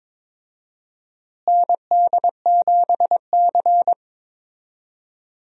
Morse code, heard as a keyed tone, about 21 wpm, 700 Hz. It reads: ND7C
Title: USA Call Signs - 4 Characters - Repeated 3x - 22wpm